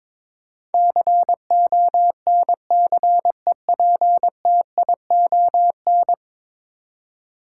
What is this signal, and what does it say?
Morse code, heard as a keyed tone, about 22 wpm, 700 Hz. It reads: CONCEPTION
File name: practice / 22wpm / 10 Letter Words - First 3 Letters Given - 22wpm